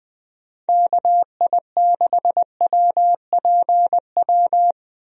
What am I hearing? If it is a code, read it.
Morse code, heard as a keyed tone, about 21 wpm, 700 Hz. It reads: KI6WPW
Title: USA Call Signs - 6 Characters - 20wpm